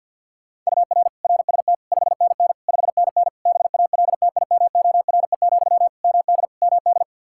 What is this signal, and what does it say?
Morse code, heard as a keyed tone, about 51 wpm, 700 Hz. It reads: UR RST 5NN 5NN BALTIMORE, MD MD
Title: Practice QSO - Exchange 1 - Beginning - 50wpm